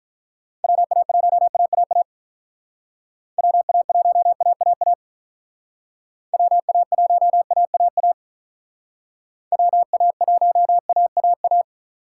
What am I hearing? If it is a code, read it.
Morse code, heard as a keyed tone, about 45 wpm, 700 Hz. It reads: WA1AAA WA1AAA WA1AAA WA1AAA